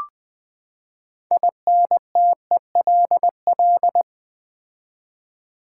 Morse code, heard as a keyed tone, about 20 wpm, 700 Hz. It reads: INTELL
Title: Build A Word - 6 Letters At A Time - 12-Letter Words - 20wpm